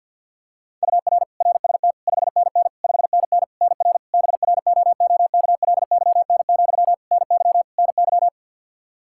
Morse code, heard as a keyed tone, about 52 wpm, 700 Hz. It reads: UR RST 5NN 5NN NR BROOKLYN, NY NY
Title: K1USN SST QSO Practice 50wpm